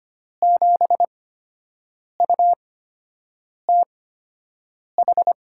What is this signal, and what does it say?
Morse code, heard as a keyed tone, about 25 wpm, spaced as for 7 wpm, 700 Hz. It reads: 7UTH
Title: USA Call Signs - 8x Character Spacing - 25wpm